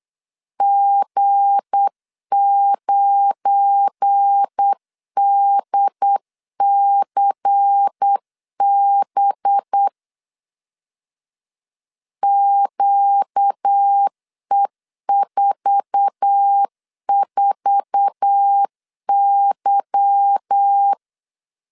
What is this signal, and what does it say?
Morse code, heard as a keyed tone, about 8 wpm, 800 Hz. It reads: G9DCB QE44Y
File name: CW / DET / 1 9 1500C S B 8wpm 5624